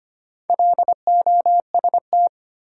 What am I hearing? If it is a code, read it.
Morse code, heard as a keyed tone, about 25 wpm, 700 Hz. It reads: LOST
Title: Sentences from Top 1400 Words - New Word Drill - 25wpm